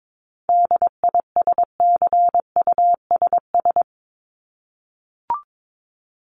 Morse code, heard as a keyed tone, about 22 wpm, 700 Hz. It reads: DISCUSS
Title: ED Suffix - Top 100 Words with Base Words 22wpm